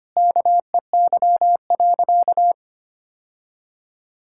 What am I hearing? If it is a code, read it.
Morse code, heard as a keyed tone, about 25 wpm, 700 Hz. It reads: KEY.
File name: practice / 25wpm / Sentences with 3-Word Phrases - 25wpm